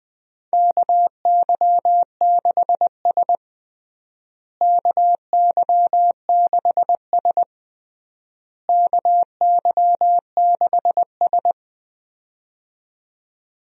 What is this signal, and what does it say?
Morse code, heard as a keyed tone, about 20 wpm, 700 Hz. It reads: KY6S KY6S KY6S
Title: USA Call Signs - 4 Characters - Repeated 3x - 20wpm